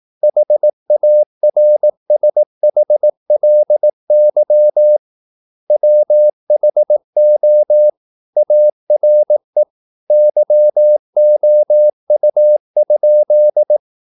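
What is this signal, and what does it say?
Morse code, heard as a keyed tone, about 18 wpm, 600 Hz. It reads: HARSHLY WHO ARE YOU?
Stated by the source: Morse practice at 18 wpm, English text, from CW med Gnister.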